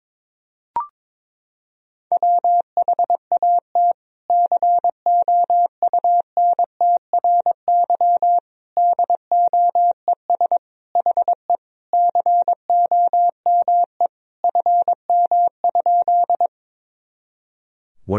Morse code, heard as a keyed tone, about 21 wpm, 700 Hz. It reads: WHAT COUNTRY DOES HE COME FM?
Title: Sentences from Top 300 Words and Abbreviations 22wpm